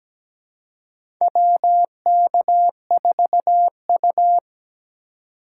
Morse code, heard as a keyed tone, about 17 wpm, 700 Hz. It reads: WK4U
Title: USA Call Signs - 4 Characters - 3x Character Spacing - 17wpm